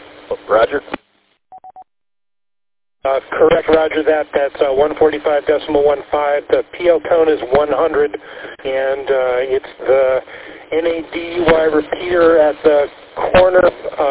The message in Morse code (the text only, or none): none